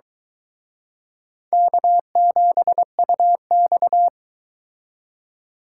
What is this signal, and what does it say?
Morse code, heard as a keyed tone, about 22 wpm, 700 Hz. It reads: K7UX